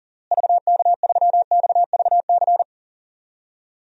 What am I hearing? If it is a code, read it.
Morse code, heard as a keyed tone, about 40 wpm, 700 Hz. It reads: VK3XVC
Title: DXCC-100 Call Signs - 6 Characters - Repeated 3x - 40wpm